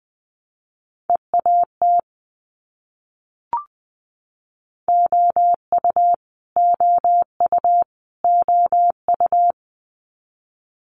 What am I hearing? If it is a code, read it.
Morse code, heard as a keyed tone, about 20 wpm, 700 Hz. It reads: EAT OU OU OU
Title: Multi-letter Phonemes Repeated 3x 20wpm